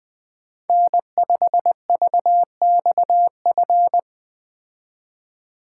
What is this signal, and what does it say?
Morse code, heard as a keyed tone, about 20 wpm, 700 Hz. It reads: N5VXF
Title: Speed Racing - USA Call Signs 17wpm